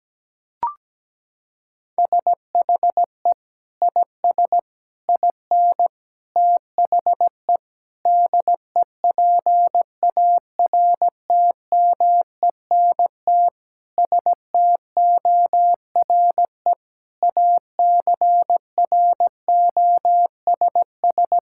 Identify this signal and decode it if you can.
Morse code, heard as a keyed tone, about 17 wpm, 700 Hz. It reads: SHE IS IN THE DEPARTMENT STORE ACROSS
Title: Sentences from Top 1000 Words - Review Part 5 - 17wpm